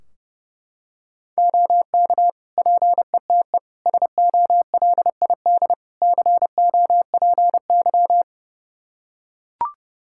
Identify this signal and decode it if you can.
Morse code, heard as a keyed tone, about 32 wpm, 700 Hz. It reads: OK PETE SOLID COPY
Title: Practice QSO - Exchange 3 - More Info - 30wpm